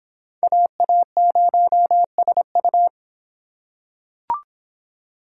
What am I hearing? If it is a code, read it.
Morse code, heard as a keyed tone, about 26 wpm, 700 Hz. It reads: AA0SU